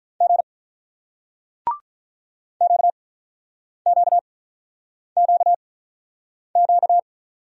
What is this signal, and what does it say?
Morse code, heard as a keyed tone, about 45 wpm, spaced as for 9 wpm, 700 Hz. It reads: G QQQQ